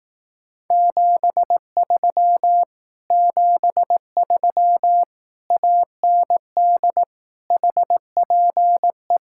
Morse code, heard as a keyed tone, about 18 wpm, 700 Hz. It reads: 73 73 AND HPE
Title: Speed-Racing Common QSO Phrases Encore - 15wpm